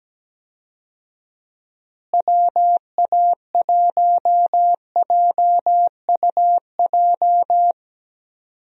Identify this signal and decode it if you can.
Morse code, heard as a keyed tone, about 17 wpm, 700 Hz. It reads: WA1JUJ
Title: USA Call Signs - 6 Characters - Repeated 3x - 17wpm